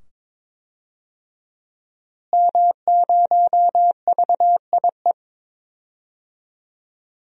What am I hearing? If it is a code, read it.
Morse code, heard as a keyed tone, about 21 wpm, 700 Hz. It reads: M0VIE
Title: DXCC-100 Call Signs - 5 Characters - Repeated 2x - 22wpm